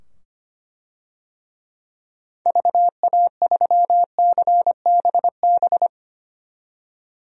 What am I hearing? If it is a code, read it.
Morse code, heard as a keyed tone, about 25 wpm, 700 Hz. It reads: VA3CBB
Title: DXCC-100 Call Signs - 6 Characters - 25wpm